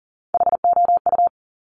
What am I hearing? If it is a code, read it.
Morse code, heard as a keyed tone, about 40 wpm, 700 Hz. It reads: HOU
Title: Top Three Letter Combinations in Words 40wpm